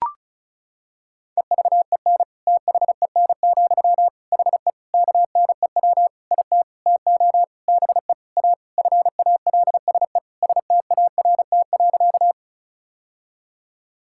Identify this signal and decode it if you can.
Morse code, heard as a keyed tone, about 36 wpm, 700 Hz. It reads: EVEN THEN, HE KNEW IT TO BE A FALSE START.
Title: Sentences from Top 1600 Words 35wpm